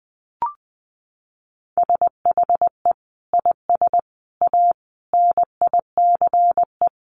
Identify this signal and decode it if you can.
Morse code, heard as a keyed tone, about 20 wpm, 700 Hz. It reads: SHE IS A NICE
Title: Sentences from Top 1100 Words 20wpm